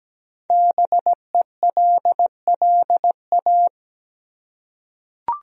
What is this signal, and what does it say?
Morse code, heard as a keyed tone, about 17 wpm, 700 Hz. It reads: BELLA
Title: USA Common Names - 17wpm